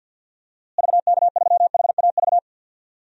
Morse code, heard as a keyed tone, about 50 wpm, 700 Hz. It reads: VK3HAV
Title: DXCC-100 Call Signs - 6 Characters - Repeated 2x - 50wpm